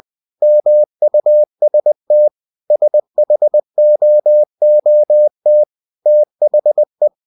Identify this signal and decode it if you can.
Morse code, heard as a keyed tone, about 21 wpm, 600 Hz. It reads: MUST SHOOT THE